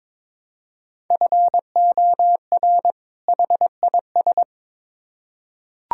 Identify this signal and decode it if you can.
Morse code, heard as a keyed tone, about 22 wpm, 700 Hz. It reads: FOR HIS
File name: Build A Phrase - 2-Word Phrases - 22wpm